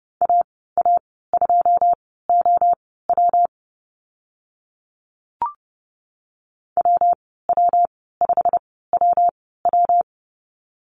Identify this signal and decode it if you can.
Morse code, heard as a keyed tone, about 30 wpm, spaced as for 17 wpm, 700 Hz. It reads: AA2OW WW5WW
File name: Lesson 19g - Letter w - Call Signs 30wpm